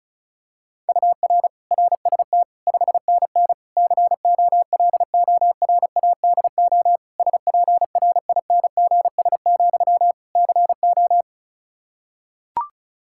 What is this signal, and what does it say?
Morse code, heard as a keyed tone, about 36 wpm, 700 Hz. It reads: UR RST 5NN COLORADO SPRINGS, CO
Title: Practice QSO - Exchange 1 - Beginning - 35wpm